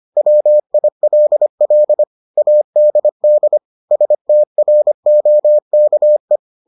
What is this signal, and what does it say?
Morse code, heard as a keyed tone, about 25 wpm, 600 Hz. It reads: WILL ADD STROKE